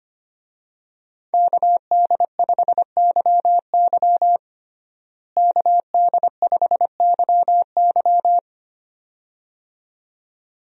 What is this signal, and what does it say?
Morse code, heard as a keyed tone, about 25 wpm, 700 Hz. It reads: KD5YY KD5YY
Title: USA Call Signs - 5 Characters - Repeated 2x - 25wpm